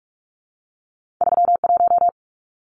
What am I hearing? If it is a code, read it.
Morse code, heard as a keyed tone, about 46 wpm, 700 Hz. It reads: 31
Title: Numbers - 2-Digit 45wpm